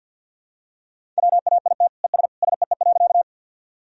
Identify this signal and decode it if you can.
Morse code, heard as a keyed tone, about 52 wpm, 700 Hz. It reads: WAIT ES SEE.
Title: Sentences from Top 500 Words and Abbreviations 50wpm